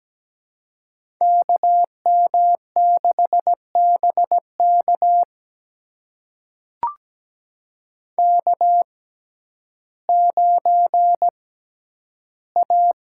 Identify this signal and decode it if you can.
Morse code, heard as a keyed tone, about 17 wpm, spaced as for 10 wpm, 700 Hz. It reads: KM6BK K9A
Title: USA Call Signs - 5 Characters - 6x Character Spacing - 17wpm